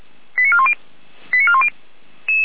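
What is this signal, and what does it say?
Five-tone selective call (ZVEI), heard as quick runs of five tones, tones of about 70 ms. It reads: 89410 89410
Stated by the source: ZVEI five-tone alert received by SDR.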